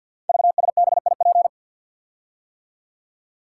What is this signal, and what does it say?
Morse code, heard as a keyed tone, about 53 wpm, 700 Hz. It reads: VSBIP
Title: Code Groups - 5 Letter:Numbers - 50wpm